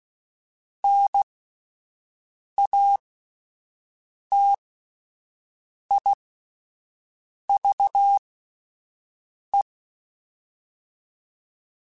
Morse code, heard as a keyed tone, about 16 wpm, spaced as for 5 wpm, 800 Hz. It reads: NATIVE
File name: Alphabetical / Alphabetical mp3s / Ns @5